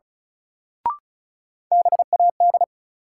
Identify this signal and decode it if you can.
Morse code, heard as a keyed tone, about 36 wpm, 700 Hz. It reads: BAD
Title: Sentences from Top 700 Words - New Word Drill - 35wpm